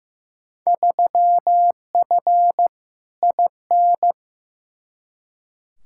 Morse code, heard as a keyed tone, about 15 wpm, 700 Hz. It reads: 3F IN
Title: Field Day Exchanges - 15wpm